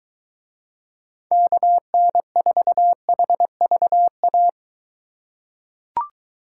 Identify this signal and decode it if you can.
Morse code, heard as a keyed tone, about 21 wpm, 700 Hz. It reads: KN4HVA